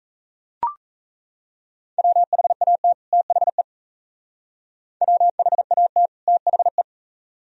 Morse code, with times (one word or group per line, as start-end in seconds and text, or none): 2.0-2.9 WHAT
3.1-3.6 THE
5.0-6.1 WHAT
6.3-6.8 THE